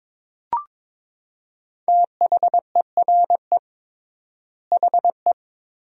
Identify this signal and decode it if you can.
Morse code, heard as a keyed tone, about 22 wpm, 700 Hz. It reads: THERE HE
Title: 3x Word-Spacing - Sentences from Top 100 Words - 22wpm